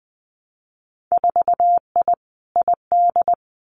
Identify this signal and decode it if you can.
Morse code, heard as a keyed tone, about 20 wpm, 700 Hz. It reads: 4I ID
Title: Winter Field Day Exchanges 20wpm